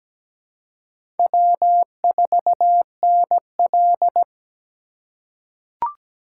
Morse code, heard as a keyed tone, about 17 wpm, 700 Hz. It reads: W4NL
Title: USA Call Signs - 4 Characters - 3x Character Spacing - 17wpm